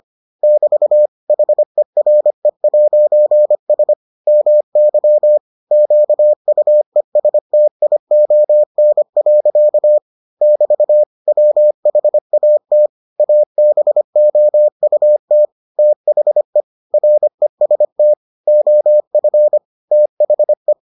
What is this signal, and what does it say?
Morse code, heard as a keyed tone, about 25 wpm, 600 Hz. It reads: <BT> HERE'S MY QUESTION. <BT> WHAT ABOUT THE REST OF THE